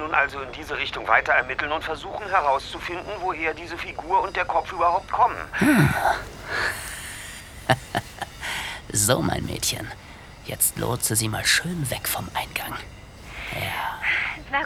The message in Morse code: none